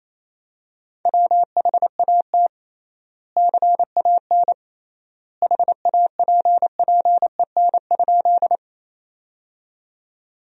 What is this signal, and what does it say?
Morse code, heard as a keyed tone, about 28 wpm, 700 Hz. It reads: WHAT CAN HAPPEN?
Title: 3x Word-Spacing - Sentences from Top 400 Words - 28wpm